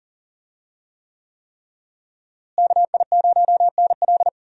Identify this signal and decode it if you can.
Morse code, heard as a keyed tone, about 40 wpm, 700 Hz. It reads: KI0NL
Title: USA Call Signs - 5 Characters - Repeated 3x - 40wpm